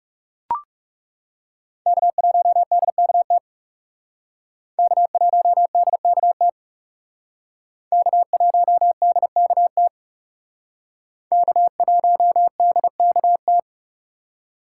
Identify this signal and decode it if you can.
Morse code, heard as a keyed tone, about 38 wpm, 700 Hz. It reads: K1DKT K1DKT K1DKT K1DKT
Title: Speed Racing - USA Call Signs 30wpm